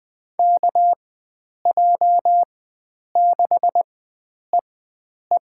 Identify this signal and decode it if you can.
Morse code, heard as a keyed tone, about 20 wpm, spaced as for 9 wpm, 700 Hz. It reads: KJ6EE
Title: USA Call Signs - 4x Character Spacing - 20wpm